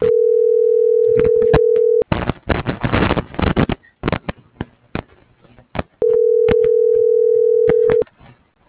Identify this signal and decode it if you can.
Ringback tone (North America), 440 + 480 Hz.